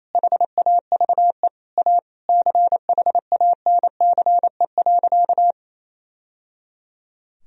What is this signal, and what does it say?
Morse code, heard as a keyed tone, about 28 wpm, 700 Hz. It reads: HAVE A CHANCE.
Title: Sentences from Top 1000 Words - Review Part 2 - 28wpm